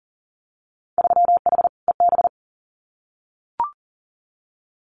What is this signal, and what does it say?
Morse code, heard as a keyed tone, about 40 wpm, 700 Hz. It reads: 3H EB
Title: Winter Field Day Exchanges Encore 40wpm